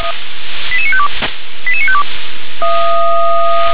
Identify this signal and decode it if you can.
Five-tone selective call (ZVEI), heard as quick runs of five tones, tones of about 70 ms. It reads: 88952 88952